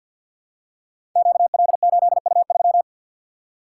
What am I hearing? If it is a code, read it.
Morse code, heard as a keyed tone, about 52 wpm, 700 Hz. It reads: QL8U3
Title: Code Groups - 5 Letter:Numbers - 50wpm